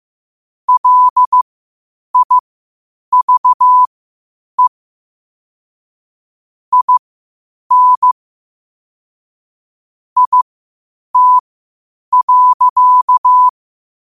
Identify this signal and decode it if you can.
Morse code, heard as a keyed tone, about 15 wpm, spaced as for 8 wpm, 1000 Hz. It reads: LIVE IN IT.
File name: qotdfiles / 8wpm